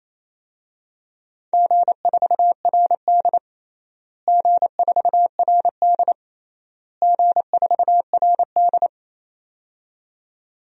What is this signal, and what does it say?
Morse code, heard as a keyed tone, about 28 wpm, 700 Hz. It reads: G4RD G4RD G4RD
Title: DXCC-100 Call Signs - 4 Characters - Repeated 3x - 28wpm